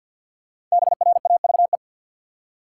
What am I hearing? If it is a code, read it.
Morse code, heard as a keyed tone, about 50 wpm, 700 Hz. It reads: BRAVE
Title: Five Letter Words - 50wpm